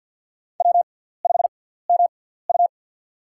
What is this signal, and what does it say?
Morse code, heard as a keyed tone, about 53 wpm, spaced as for 17 wpm, 700 Hz. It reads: W5MU
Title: USA Call Signs - 4 Characters - 6x Character Spacing - 50wpm